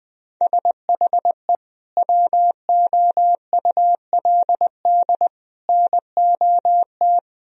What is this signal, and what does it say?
Morse code, heard as a keyed tone, about 20 wpm, 700 Hz. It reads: SHE WOULD NOT
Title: Sentences from Top 1000 Words - Review Part 2 - 20wpm